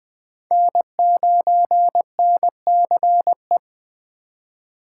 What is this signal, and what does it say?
Morse code, heard as a keyed tone, about 20 wpm, 700 Hz. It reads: N9NCE
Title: Speed Racing - USA 1x3 Call Signs 17wpm